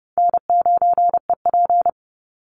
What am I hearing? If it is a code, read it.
Morse code, heard as a keyed tone, about 30 wpm, 700 Hz. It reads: N9EP
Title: Easy Call Signs Repeated 2x - 30wpm